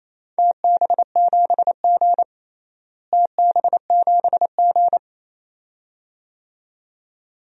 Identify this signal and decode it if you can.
Morse code, heard as a keyed tone, about 28 wpm, 700 Hz. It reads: TB7G TB7G
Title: DXCC-100 Call Signs - 4 Characters - Repeated 2x - 28wpm